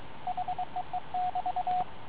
Morse code, heard as a keyed tone, about 22 wpm, 730 Hz, with radio noise. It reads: HEE<BT>